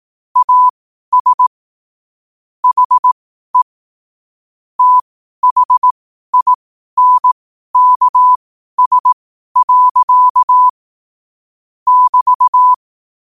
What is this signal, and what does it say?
Morse code, heard as a keyed tone, about 18 wpm, spaced as for 12 wpm, 1000 Hz. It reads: AS HE THINKS. <BT>
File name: 12wpm